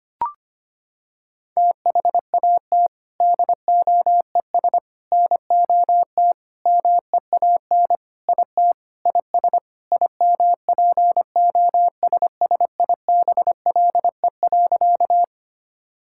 Morse code, heard as a keyed tone, about 25 wpm, 700 Hz. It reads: THAT DOES NOT MEAN IT IS IMPOSSIBLE.
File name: practice / 25wpm / Sentences with 4-Word Phrases - 25wpm